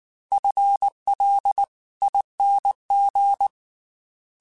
Morse code, heard as a keyed tone, about 19 wpm, 800 Hz. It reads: FLING
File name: FL / FL mp3s / FL Prefixes 3x2 sp @18